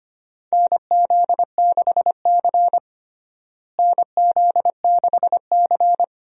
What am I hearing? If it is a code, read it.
Morse code, heard as a keyed tone, about 25 wpm, 700 Hz. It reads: NZ6C NZ6C
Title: USA Call Signs - 4 Characters - Repeated 2x - 25wpm